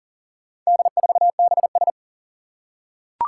Morse code, heard as a keyed tone, about 40 wpm, 700 Hz. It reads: D4BS
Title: Four Challenging Letter-Number Combinations 40wpm